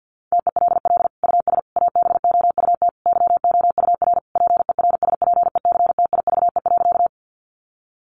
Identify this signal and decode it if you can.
Morse code, heard as a keyed tone, about 50 wpm, 700 Hz. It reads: TELL US ABOUT YOUR PERSPECTIVE.
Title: Sentences from Top 1200 Words 50wpm